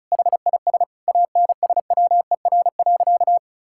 Morse code, heard as a keyed tone, about 36 wpm, 700 Hz. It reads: HIS ANSWER.
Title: Sentences from Top 1600 Words 35wpm